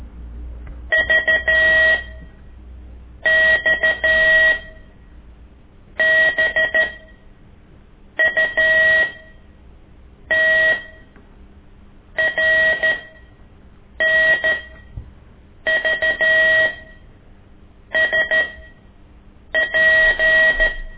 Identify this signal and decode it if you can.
Morse code, heard as a keyed tone, about 13 wpm, spaced as for 6 wpm, 1900 Hz. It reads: VXBUT RNVSP
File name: ik1weg2 / CORSI CW